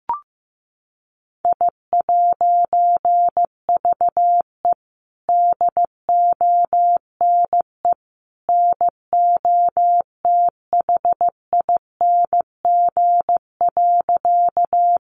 Morse code, heard as a keyed tone, about 15 wpm, 700 Hz. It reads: I'VE DONE NOTHING.
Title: A Sound Of Thunder 15wpm